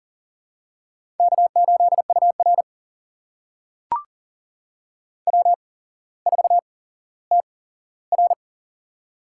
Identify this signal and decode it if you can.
Morse code, heard as a keyed tone, about 40 wpm, spaced as for 17 wpm, 700 Hz. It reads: K8UR W4TR
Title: USA Call Signs - 8x Character Spacing - 40wpm